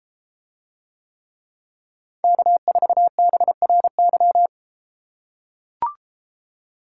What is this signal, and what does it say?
Morse code, heard as a keyed tone, about 33 wpm, 700 Hz. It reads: K4BRY